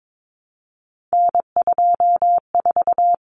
Morse code, heard as a keyed tone, about 22 wpm, 700 Hz. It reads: N24
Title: Common Cut Numbers - 3-Digit 22wpm